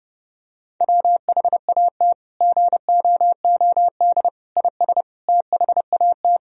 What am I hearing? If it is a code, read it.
Morse code, heard as a keyed tone, about 30 wpm, 700 Hz. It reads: WHAT GOOD IS THAT
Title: Sentences from Top 1000 Words - Review Part 3 - 30wpm